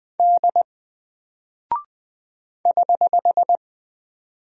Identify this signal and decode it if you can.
Morse code, heard as a keyed tone, about 20 wpm, 700 Hz. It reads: D <HH>
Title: Single Character (Letter:Number:Punctuation:Prosign) Alt 20wpm